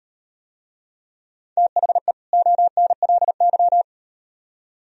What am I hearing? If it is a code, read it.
Morse code, heard as a keyed tone, about 38 wpm, 700 Hz. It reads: THE ONLY